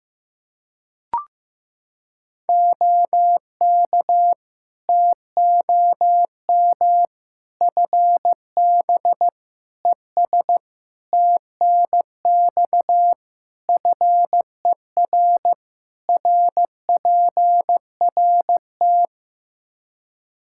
Morse code, heard as a keyed tone, about 15 wpm, 700 Hz. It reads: OK TOM FB ES TNX FER RPRT
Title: Practice QSO - Exchange 2 - Info - 15wpm